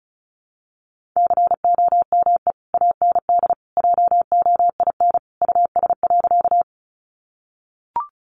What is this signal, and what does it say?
Morse code, heard as a keyed tone, about 34 wpm, 700 Hz. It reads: COME AND JOIN US.
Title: Sentences from Top 1400 Words 35wpm